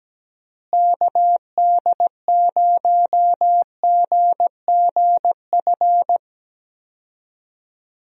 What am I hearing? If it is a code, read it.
Morse code, heard as a keyed tone, about 17 wpm, 700 Hz. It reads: KD0GGF
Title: USA Call Signs - 6 Characters - Repeated 2x - 17wpm